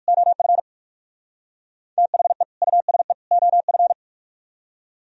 Morse code, heard as a keyed tone, about 55 wpm, 700 Hz. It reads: OF THE USE OF
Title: Speed-Racing - Sets of 3 Words 45wpm